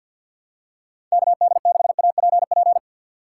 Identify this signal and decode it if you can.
Morse code, heard as a keyed tone, about 52 wpm, 690 Hz. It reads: KD6APP
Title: USA Call Signs - 6 Characters - 3x Character Spacing - 50wpm